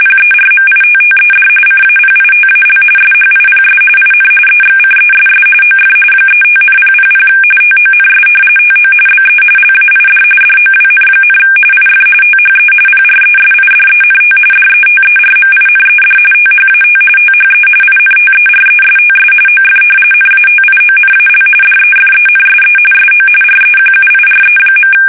RTTY, about 45.45 baud. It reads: PERMITTED CAPITAL SHIPS IN EXCESS OF 35,000 TONS TO BE LAID DOWN, IF THE PRESIDENT DETERMINED THAT THE INTERESTS OF NATIONAL DEFENSE SO REQUIRE.
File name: fsk850 / battleshipnj